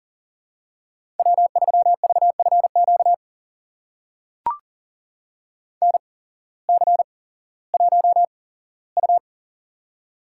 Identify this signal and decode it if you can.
Morse code, heard as a keyed tone, about 38 wpm, spaced as for 18 wpm, 700 Hz. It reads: W3VFQ NC1U
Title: USA Call Signs - 8x Character Spacing - 40wpm